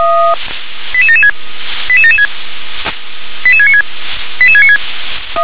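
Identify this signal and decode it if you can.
Five-tone selective call (ZVEI), heard as quick runs of five tones, tones of about 70 ms. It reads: 88776 88776 88696 88696
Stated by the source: ZVEI five-tone selective call 88559 received on an SDR.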